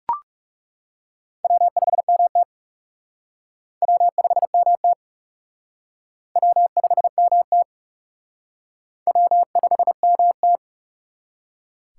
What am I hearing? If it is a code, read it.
Morse code, heard as a keyed tone, about 39 wpm, 700 Hz. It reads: W5MT W5MT W5MT W5MT